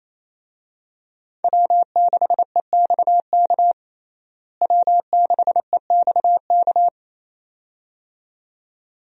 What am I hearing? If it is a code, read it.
Morse code, heard as a keyed tone, about 28 wpm, 700 Hz. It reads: W6EXK W6EXK
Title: USA Call Signs - 5 Characters - Repeated 2x - 28wpm